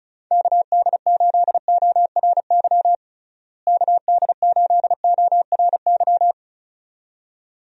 Morse code, heard as a keyed tone, about 34 wpm, 700 Hz. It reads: KD8ORY KD8ORY
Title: USA Call Signs - 6 Characters - Repeated 2x - 35wpm